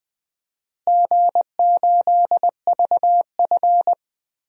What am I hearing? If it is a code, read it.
Morse code, heard as a keyed tone, about 20 wpm, 700 Hz. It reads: G8VF
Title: DXCC-100 Call Signs - 4 Characters - Repeated 2x - 20wpm